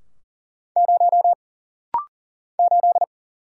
Morse code, heard as a keyed tone, about 40 wpm, 700 Hz. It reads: 0 8
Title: Single Letter-Number - Rapid-Fire 40wpm